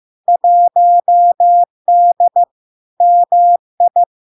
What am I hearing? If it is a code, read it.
Morse code, heard as a keyed tone, about 15 wpm, 700 Hz. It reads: 1D MI
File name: LARK-FD-2022-Log-in-CW-15wpm-700-Hz